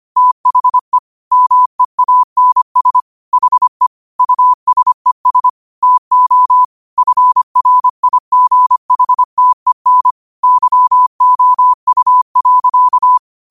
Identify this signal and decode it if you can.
Morse code, heard as a keyed tone, about 25 wpm, 1000 Hz. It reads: THE MEANS HE USES TO FRIGHTEN YOU.
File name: qotdfiles / 25wpm